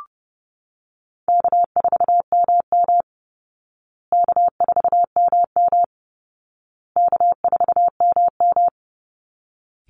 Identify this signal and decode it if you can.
Morse code, heard as a keyed tone, about 30 wpm, 700 Hz. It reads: K4MM K4MM K4MM
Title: Easy Call Signs Repeated 3x - 30wpm